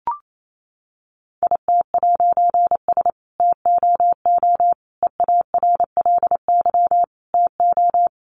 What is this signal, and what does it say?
Morse code, heard as a keyed tone, about 28 wpm, 700 Hz. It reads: IT'S TOO EARLY TO
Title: A Sound Of Thunder 28wpm